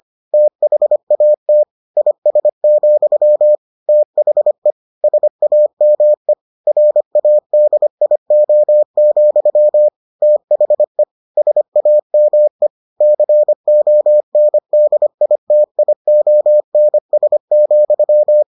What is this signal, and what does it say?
Morse code, heard as a keyed tone, about 25 wpm, 600 Hz. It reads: THAT IS, THE SAME RADIO, THE SAME CONDITIONS,